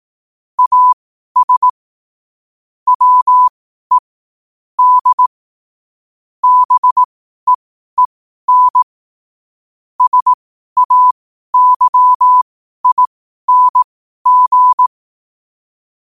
Morse code, heard as a keyed tone, about 18 wpm, spaced as for 12 wpm, 1000 Hz. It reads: AS WE D BEEN SAYING
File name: qotdfiles / 12wpm